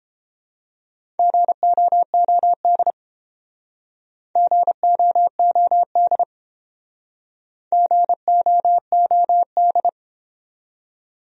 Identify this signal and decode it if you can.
Morse code, heard as a keyed tone, about 32 wpm, 700 Hz. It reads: GOOD GOOD GOOD